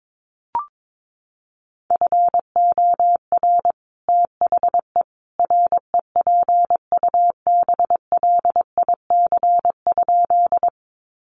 Morse code, heard as a keyed tone, about 22 wpm, 700 Hz. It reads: FOR THE REPUBLIC?
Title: Sentences from Top 1500 Words 22wpm